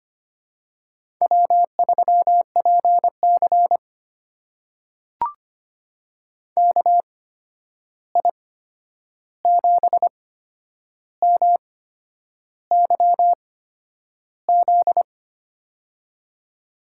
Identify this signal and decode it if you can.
Morse code, heard as a keyed tone, about 25 wpm, spaced as for 9 wpm, 700 Hz. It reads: W3PC KI7MYZ
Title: USA Call Signs - 8x Character Spacing - 25wpm